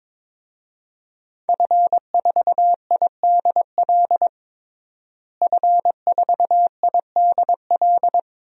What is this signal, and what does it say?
Morse code, heard as a keyed tone, about 22 wpm, 700 Hz. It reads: F4IDL F4IDL
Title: DXCC-100 Call Signs - 5 Characters - Repeated 2x - 22wpm